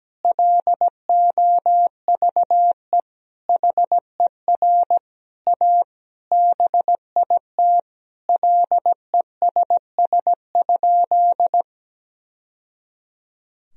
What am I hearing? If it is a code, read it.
Morse code, heard as a keyed tone, about 17 wpm, 700 Hz. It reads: LOVE HER A BIT LESS?
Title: Sentences from Top 1000 Words - Review Part 2 - 17wpm